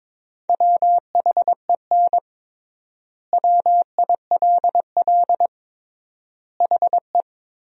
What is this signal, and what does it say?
Morse code, heard as a keyed tone, about 22 wpm, 700 Hz. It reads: WHEN WILL HE
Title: 3x Word-Spacing - Sentences from Top 200 Words - 22wpm